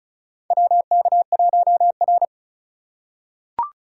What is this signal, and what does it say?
Morse code, heard as a keyed tone, about 34 wpm, 700 Hz. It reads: WK1R
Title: USA Call Signs - 2x Character Spacing - 35wpm